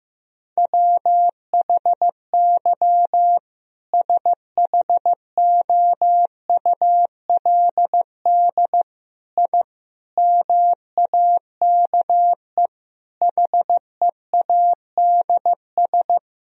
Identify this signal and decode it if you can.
Morse code, heard as a keyed tone, about 15 wpm, 700 Hz. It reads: WHY SHOULD I MAKE HEADS
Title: Sentences from Top 1000 Words - Review Part 3 - 15wpm